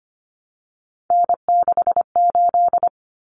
Morse code, heard as a keyed tone, about 25 wpm, 700 Hz. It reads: N68
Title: Common Cut Numbers - 3-Digit 25wpm